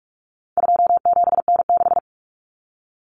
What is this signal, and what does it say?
Morse code, heard as a keyed tone, about 44 wpm, 700 Hz. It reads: 27N6
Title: Common Cut Numbers - 4-Digit 45wpm